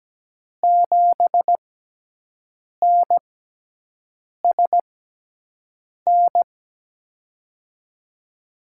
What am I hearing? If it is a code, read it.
Morse code, heard as a keyed tone, about 17 wpm, spaced as for 6 wpm, 700 Hz. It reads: 7NSN
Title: USA Call Signs - 5 Characters - 6x Character Spacing - 17wpm